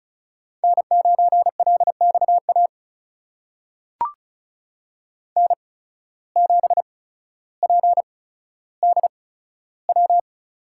Morse code, heard as a keyed tone, about 36 wpm, spaced as for 15 wpm, 700 Hz. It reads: N9LXA N7PDW
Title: USA Call Signs - 5 Characters - 8x Character Spacing - 35wpm